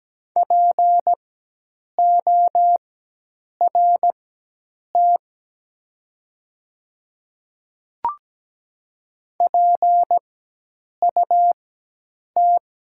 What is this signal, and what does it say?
Morse code, heard as a keyed tone, about 17 wpm, spaced as for 8 wpm, 700 Hz. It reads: PORT PUT
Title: 4x Character Spacing - Top 200 Words - 17wpm